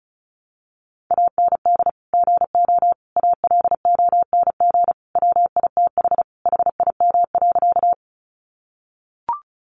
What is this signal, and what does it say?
Morse code, heard as a keyed tone, about 34 wpm, 700 Hz. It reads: AND GO ALONG WITH HIM.
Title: Sentences from Top 700 Words 35wpm